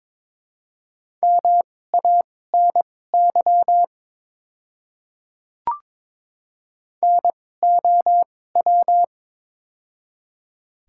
Morse code, heard as a keyed tone, about 22 wpm, spaced as for 16 wpm, 690 Hz. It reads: MANY NOW
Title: 2x Character Spacing - Top 100 Words - 22wpm